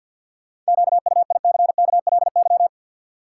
Morse code, heard as a keyed tone, about 51 wpm, 700 Hz. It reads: QUICKLY
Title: Sentences from Top 700 Words - New Word Drill - 50wpm